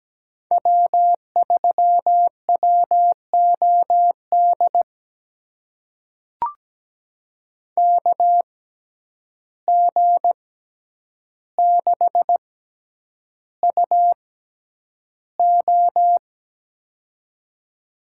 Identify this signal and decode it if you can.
Morse code, heard as a keyed tone, about 17 wpm, spaced as for 9 wpm, 700 Hz. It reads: W3WOD KG6UO
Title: USA Call Signs - 5 Characters - 6x Character Spacing - 17wpm